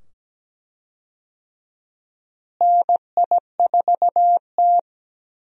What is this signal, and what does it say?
Morse code, heard as a keyed tone, about 17 wpm, 700 Hz. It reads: NI4T
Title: USA Call Signs - 4 Characters - Repeated 2x - 17wpm